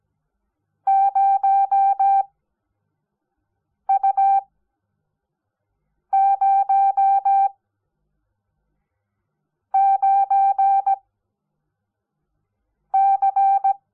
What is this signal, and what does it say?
Morse code, heard as a keyed tone, about 17 wpm, spaced as for 4 wpm, 780 Hz. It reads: 0U09C
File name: Onlinelehrgang / morsen / DJ4UF Morsen05